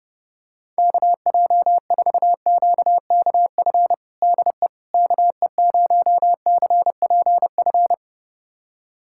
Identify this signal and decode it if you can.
Morse code, heard as a keyed tone, about 32 wpm, 700 Hz. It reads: KJ4QKF DE KE0CPF
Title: USA Callsign DE Callsign - 6 Characters - 30wpm